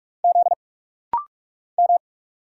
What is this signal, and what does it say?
Morse code, heard as a keyed tone, about 44 wpm, 700 Hz. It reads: Z M
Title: Single Letters - Rapid-Fire 45wpm